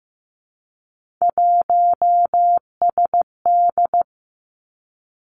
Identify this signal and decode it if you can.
Morse code, heard as a keyed tone, about 15 wpm, 700 Hz. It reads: 1SD
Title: Three Challenging Letter-Number Combinations 15wpm